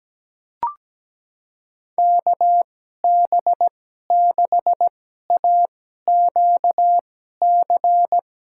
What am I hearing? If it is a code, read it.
Morse code, heard as a keyed tone, about 17 wpm, spaced as for 12 wpm, 700 Hz. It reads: KB6AQC
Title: USA Call Signs - 6 Characters - 2x Character Spacing - 17wpm